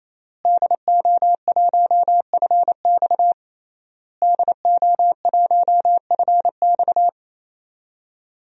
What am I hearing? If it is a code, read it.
Morse code, heard as a keyed tone, about 28 wpm, 700 Hz. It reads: DO1FX DO1FX
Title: DXCC-100 Call Signs - 5 Characters - Repeated 3x - 28wpm